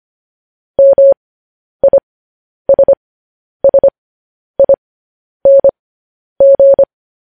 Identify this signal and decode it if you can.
Morse code, heard as a keyed tone, about 25 wpm, spaced as for 10 wpm, 550 Hz. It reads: MISSING
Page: News Headlines in Morse Code at 10 WPM